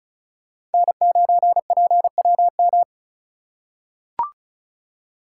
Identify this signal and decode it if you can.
Morse code, heard as a keyed tone, about 34 wpm, 700 Hz. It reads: N9PWM
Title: USA Call Signs - 5 Characters - 4x Character Spacing - 35wpm